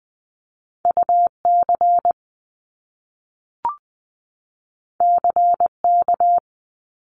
Morse code, heard as a keyed tone, about 20 wpm, 700 Hz. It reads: UC CK
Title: Top Two Letter Combinations in Words 20wpm